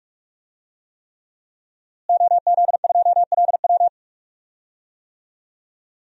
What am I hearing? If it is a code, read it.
Morse code, heard as a keyed tone, about 46 wpm, 700 Hz. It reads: OZ2LW
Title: DXCC-100 Call Signs - 5 Characters - Repeated 3x - 45wpm